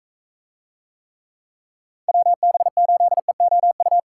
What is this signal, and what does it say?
Morse code, heard as a keyed tone, about 43 wpm, 700 Hz. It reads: WB8EOU